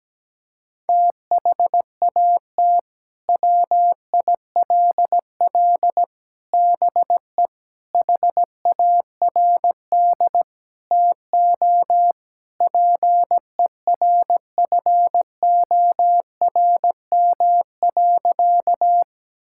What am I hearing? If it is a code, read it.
Morse code, heard as a keyed tone, about 17 wpm, 700 Hz. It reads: THAT WILL BE HARD TO PERFORM.
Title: Sentences from Top 1000 Words - Review Part 5 - 17wpm